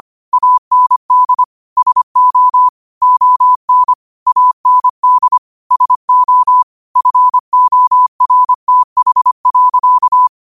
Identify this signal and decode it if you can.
Morse code, heard as a keyed tone, about 25 wpm, 1000 Hz. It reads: AND SO ON AND SO FORTH.
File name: qotdfiles / 25wpm